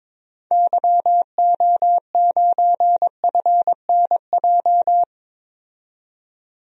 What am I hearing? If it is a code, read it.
Morse code, heard as a keyed tone, about 22 wpm, 700 Hz. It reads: YO9FNJ
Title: DXCC-100 Call Signs - 6 Characters - 22wpm